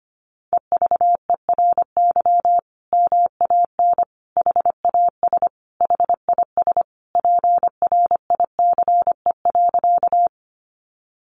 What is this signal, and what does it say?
Morse code, heard as a keyed tone, about 25 wpm, 700 Hz. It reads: EVERY MAN HAS HIS PRICE.
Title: Sentences from Top 600 Words 25wpm